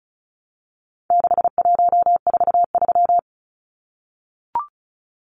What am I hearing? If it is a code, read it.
Morse code, heard as a keyed tone, about 34 wpm, 700 Hz. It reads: 6143
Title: Numbers - 4-Digit 35wpm